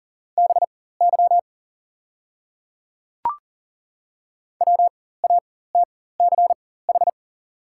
Morse code, heard as a keyed tone, about 40 wpm, spaced as for 19 wpm, 700 Hz. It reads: BY WATCH
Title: 4x Character Spacing - Top 300 Words - 40wpm